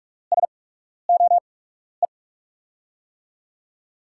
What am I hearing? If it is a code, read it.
Morse code, heard as a keyed tone, about 44 wpm, spaced as for 12 wpm, 700 Hz. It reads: SOE
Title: USA Call Signs - 6 Characters - 8x Character Spacing - 45wpm